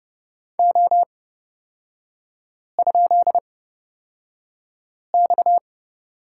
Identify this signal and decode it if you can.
Morse code, heard as a keyed tone, about 30 wpm, spaced as for 5 wpm, 700 Hz. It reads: O?X
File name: Lesson 37e - Letter x - Three Characters 30wpm